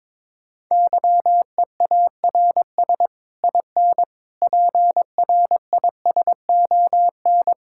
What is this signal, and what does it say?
Morse code, heard as a keyed tone, about 22 wpm, 700 Hz. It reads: YEARS IN PRISON
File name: Sets of 9 Words - 22wpm